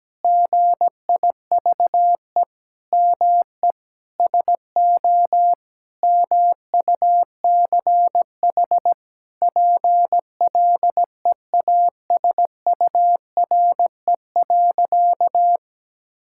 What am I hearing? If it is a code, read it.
Morse code, heard as a keyed tone, about 17 wpm, 700 Hz. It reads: GIVE ME SO MUCH PLEASURE.
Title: Sentences from Top 1600 Words 17wpm